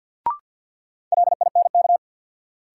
Morse code, heard as a keyed tone, about 51 wpm, 700 Hz. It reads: LINK
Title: Sentences from Top 1500 Words - New Word Drill Alt - 50wpm